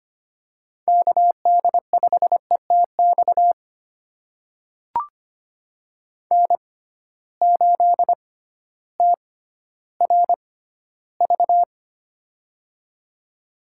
Morse code, heard as a keyed tone, about 25 wpm, spaced as for 13 wpm, 700 Hz. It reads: KD5ETX N8TRV
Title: USA Call Signs - 6x Character Spacing - 25wpm